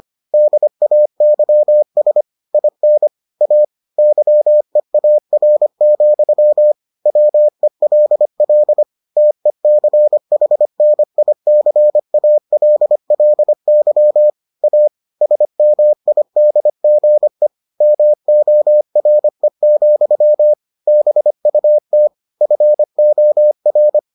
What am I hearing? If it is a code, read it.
Morse code, heard as a keyed tone, about 25 wpm, 600 Hz. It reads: DAYS IN A YEAR, WELL TECHNICALLY A SMIDGE MORE, BUT FOR